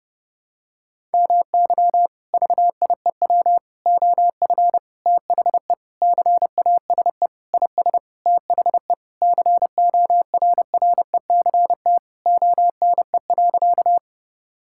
Morse code, heard as a keyed tone, about 32 wpm, 700 Hz. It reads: MY VIEW OF THE CASE IS THE CORRECT ONE.
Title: Sentences from Top 1000 Words - Review Part 6 - 30wpm